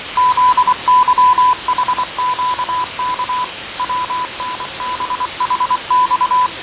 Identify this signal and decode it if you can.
Morse code, heard as a keyed tone, about 24 wpm, 1000 Hz. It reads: ZYHQK WNBHX
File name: 6824usb